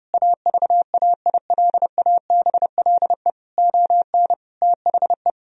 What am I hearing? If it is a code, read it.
Morse code, heard as a keyed tone, about 32 wpm, 700 Hz. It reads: AVAILABLE ON THE
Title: Sentences from Top 1000 Words - Review Part 3 - 30wpm